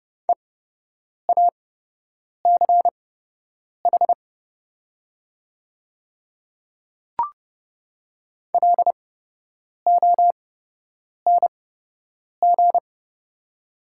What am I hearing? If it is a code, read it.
Morse code, heard as a keyed tone, about 29 wpm, spaced as for 8 wpm, 690 Hz. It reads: EACH LONG
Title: 8x Character Spacing - Top 500 Words - 30wpm